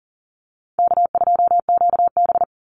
Morse code, heard as a keyed tone, about 40 wpm, 700 Hz. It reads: K2QB
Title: Easy Call Signs - 40wpm